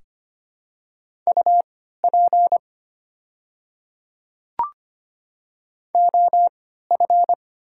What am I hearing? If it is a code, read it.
Morse code, heard as a keyed tone, about 25 wpm, spaced as for 14 wpm, 700 Hz. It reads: UP OF
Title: 3x Character Spacing - Top 100 Words - 25wpm